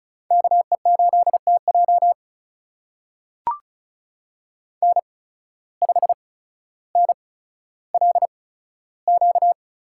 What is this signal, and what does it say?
Morse code, heard as a keyed tone, about 36 wpm, spaced as for 15 wpm, 700 Hz. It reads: KE8TJ N5NLQ
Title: USA Call Signs - 5 Characters - 8x Character Spacing - 35wpm